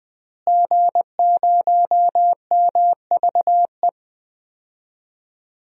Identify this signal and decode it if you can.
Morse code, heard as a keyed tone, about 20 wpm, 700 Hz. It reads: G0MVE
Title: DXCC-100 Call Signs - 5 Characters - Repeated 3x - 20wpm